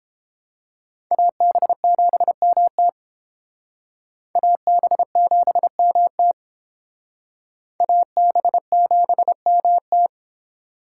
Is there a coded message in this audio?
Yes: Morse code from 1.1 to 10.1 s, heard as a keyed tone, about 31 wpm, 700 Hz. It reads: AB7MT AB7MT AB7MT